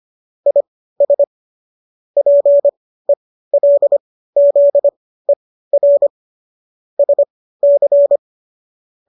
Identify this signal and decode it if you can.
Morse code, heard as a keyed tone, about 25 wpm, spaced as for 15 wpm, 570 Hz. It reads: IS PELZER SC